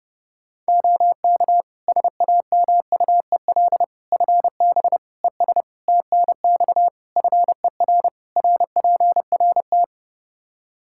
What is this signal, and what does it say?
Morse code, heard as a keyed tone, about 30 wpm, 700 Hz. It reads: OK SAMUEL FB ES TNX FER RPRT
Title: Practice QSO - Exchange 2 - Info - 30wpm